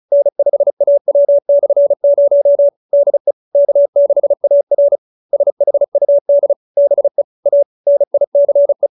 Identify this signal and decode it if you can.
Morse code, heard as a keyed tone, about 35 wpm, 570 Hz. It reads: N5AW/0 DE K6AR SHUD BE A NICE
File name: QSO312 35